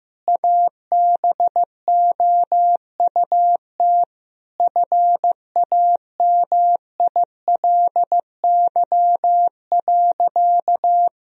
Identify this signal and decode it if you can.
Morse code, heard as a keyed tone, about 15 wpm, 700 Hz. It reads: ABOUT FAMILY.
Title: Sentences from Top 1000 Words - Review Part 3 - 15wpm